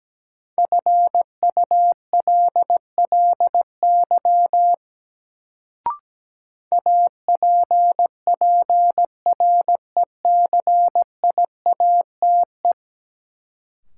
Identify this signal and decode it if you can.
Morse code, heard as a keyed tone, about 17 wpm, 700 Hz. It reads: FULLY APPRECIATE
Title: Sentences from Top 1300 Words - New Word Drill - 17wpm